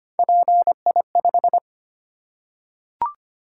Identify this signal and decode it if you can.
Morse code, heard as a keyed tone, about 25 wpm, 700 Hz. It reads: PI5
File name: Three Letter-Number Combinations - 25wpm